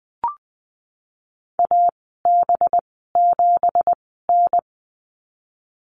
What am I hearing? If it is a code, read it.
Morse code, heard as a keyed tone, about 20 wpm, spaced as for 15 wpm, 700 Hz. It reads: AB7N
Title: 186 Lesson 30g - Letter b - Call Signs 20wpm